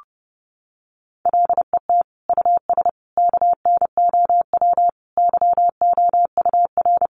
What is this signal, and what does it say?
Morse code, heard as a keyed tone, about 30 wpm, 700 Hz. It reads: LET US KNOW YOUR
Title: Sentences from Top 1400 Words 30wpm